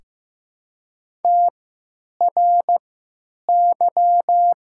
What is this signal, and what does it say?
Morse code, heard as a keyed tone, about 15 wpm, spaced as for 9 wpm, 700 Hz. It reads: TRY
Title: 3x Character Spacing - Top 500 Words - 15wpm